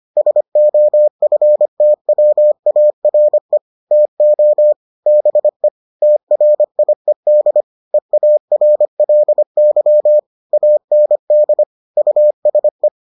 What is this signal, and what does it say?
Morse code, heard as a keyed tone, about 25 wpm, 600 Hz. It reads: SOFTWARE TO BE TRIED EARLY AND USE